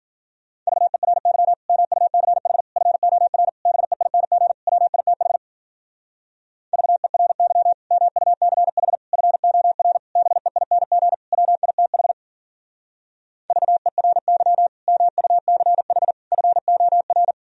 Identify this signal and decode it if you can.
Morse code, heard as a keyed tone, about 51 wpm, 700 Hz. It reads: VERY MUCH FOR BEING WITH VERY MUCH FOR BEING WITH VERY MUCH FOR